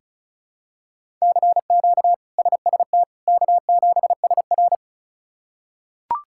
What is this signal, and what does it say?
Morse code, heard as a keyed tone, about 36 wpm, 700 Hz. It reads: CQ SST K7SR
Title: K1USN SST QSO Practice - 35wpm